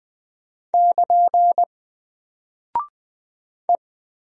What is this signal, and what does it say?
Morse code, heard as a keyed tone, about 20 wpm, 700 Hz. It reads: <KN> E
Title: Single Character (Letter:Number:Punctuation:Prosign) 20wpm